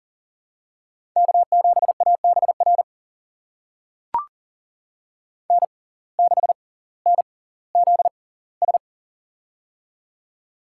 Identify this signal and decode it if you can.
Morse code, heard as a keyed tone, about 40 wpm, spaced as for 20 wpm, 700 Hz. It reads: K7ABR N6NZS
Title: USA Call Signs - 5 Characters - 6x Character Spacing - 40wpm